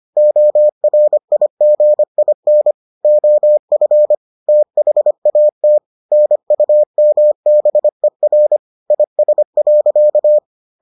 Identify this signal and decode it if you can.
Morse code, heard as a keyed tone, about 25 wpm, 600 Hz. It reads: ORIGIN OF THAT NUMBER IS.